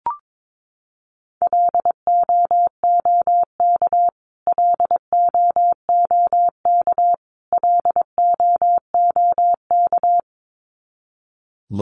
Morse code, heard as a keyed tone, about 22 wpm, 700 Hz. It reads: LOOK LOOK LOOK
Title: Top 300 Words Repeated 22wpm